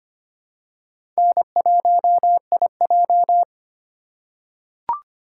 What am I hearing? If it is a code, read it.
Morse code, heard as a keyed tone, about 25 wpm, 700 Hz. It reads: N1IJ
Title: USA Call Signs - 4x Character Spacing - 25wpm